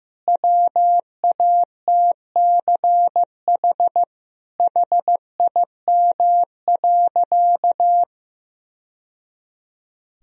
Morse code, heard as a keyed tone, about 15 wpm, 700 Hz. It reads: WATCH HIM.
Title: Sentences from Top 1000 Words - Review Part 1 - 15wpm